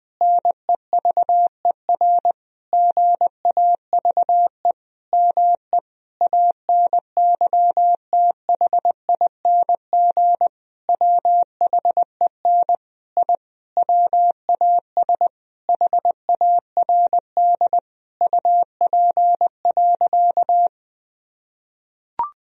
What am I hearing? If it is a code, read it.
Morse code, heard as a keyed tone, about 20 wpm, 700 Hz. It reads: NEVER GAVE ME ANYTHING WHEN I WAS HARD UP.
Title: Sentences from Top 1000 Words - Review Part 1 - 20wpm